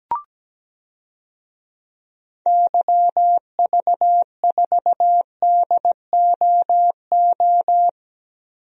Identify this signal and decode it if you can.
Morse code, heard as a keyed tone, about 17 wpm, 700 Hz. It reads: YV4DOO
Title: DXCC-100 Call Signs - 6 Characters - Repeated 3x - 17wpm